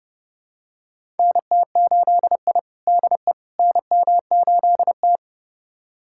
Morse code, heard as a keyed tone, about 30 wpm, 700 Hz. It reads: NT8I DE NM8T
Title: USA Callsign DE Callsign - 4 Characters - 30wpm